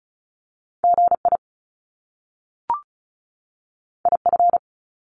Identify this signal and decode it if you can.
Morse code, heard as a keyed tone, about 36 wpm, 700 Hz. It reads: GI IF
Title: Top Two Letter Combinations in Words 35wpm